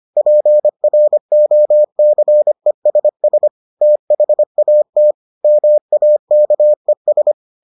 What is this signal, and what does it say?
Morse code, heard as a keyed tone, about 25 wpm, 600 Hz. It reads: PROCESS THAT MAKES